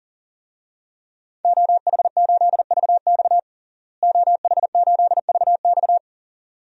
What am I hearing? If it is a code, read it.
Morse code, heard as a keyed tone, about 37 wpm, 700 Hz. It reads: OH8VX OH8VX
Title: DXCC-100 Call Signs - 5 Characters - Repeated 2x - 40wpm